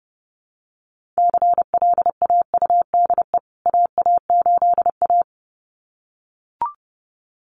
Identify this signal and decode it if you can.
Morse code, heard as a keyed tone, about 30 wpm, 700 Hz. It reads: CLAUDE AA8A
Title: CWT Exchange 30wpm